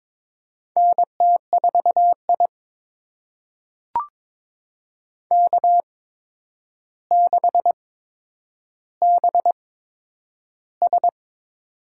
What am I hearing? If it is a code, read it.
Morse code, heard as a keyed tone, about 21 wpm, spaced as for 9 wpm, 700 Hz. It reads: NT4I K6BS
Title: USA Call Signs - 4 Characters - 8x Character Spacing - 22wpm